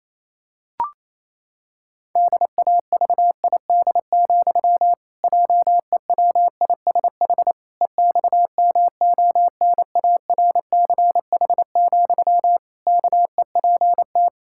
Morse code, heard as a keyed tone, about 28 wpm, 700 Hz. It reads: DAVID, JEWISH EXMONARCH, KEPT
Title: Pangram Challenge v2 - 28wpm